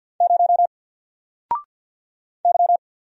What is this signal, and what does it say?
Morse code, heard as a keyed tone, about 48 wpm, 700 Hz. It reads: 0 Y
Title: Single Letter-Number Alt 50wpm